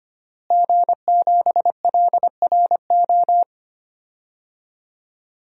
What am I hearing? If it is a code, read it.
Morse code, heard as a keyed tone, about 25 wpm, 700 Hz. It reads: G7LRO
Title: DXCC-100 Call Signs - 5 Characters - Repeated 3x - 25wpm